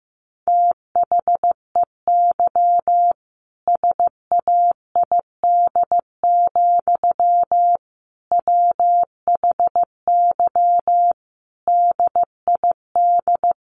Morse code, heard as a keyed tone, about 15 wpm, 700 Hz. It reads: THEY SAID, WHY DID
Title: Sentences from Top 1200 Words 15wpm